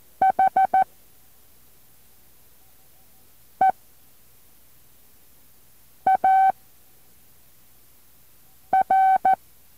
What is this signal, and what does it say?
Morse code, heard as a keyed tone, about 14 wpm, spaced as for 3 wpm, 760 Hz. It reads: HEAR
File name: intl morse code2